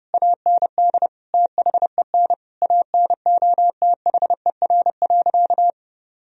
Morse code, heard as a keyed tone, about 32 wpm, 700 Hz. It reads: AND THEN ANOTHER.
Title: Sentences from Top 1000 Words - Review Part 2 - 30wpm